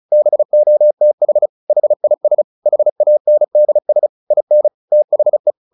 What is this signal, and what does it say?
Morse code, heard as a keyed tone, about 34 wpm, 600 Hz. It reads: BOTH HIS HANDS IN THE
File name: the old man and the sea 2 35wpm